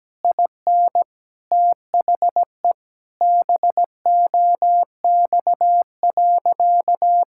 Morse code, heard as a keyed tone, about 17 wpm, 700 Hz. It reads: IN THE BOX.
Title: Sentences from Top 1000 Words - Review Part 5 - 17wpm